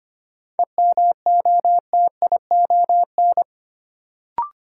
Morse code, heard as a keyed tone, about 25 wpm, 700 Hz. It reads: EMOTION
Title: Sentences from Top 1600 Words - New Word Drill - 25wpm